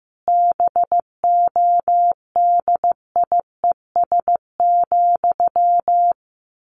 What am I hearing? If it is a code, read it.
Morse code, heard as a keyed tone, about 15 wpm, 700 Hz. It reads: BODIES,
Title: A Sound Of Thunder 15wpm